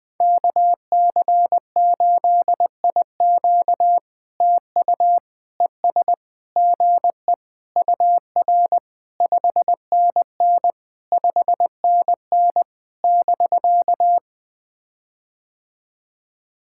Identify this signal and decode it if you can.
Morse code, heard as a keyed tone, about 21 wpm, 700 Hz. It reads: KC8IQ TU ES GE UR 5NN 5NN <BK>
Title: POTA QSO Practice - 20wpm